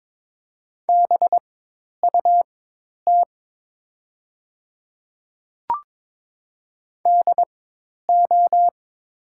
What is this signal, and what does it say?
Morse code, heard as a keyed tone, about 22 wpm, spaced as for 10 wpm, 700 Hz. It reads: BUT DO
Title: 4x Character Spacing - Top 200 Words - 22wpm